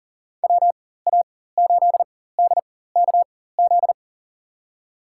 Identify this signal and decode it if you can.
Morse code, heard as a keyed tone, about 39 wpm, spaced as for 19 wpm, 700 Hz. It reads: WA8DKZ
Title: USA Call Signs - 6 Characters - 4x Character Spacing - 40wpm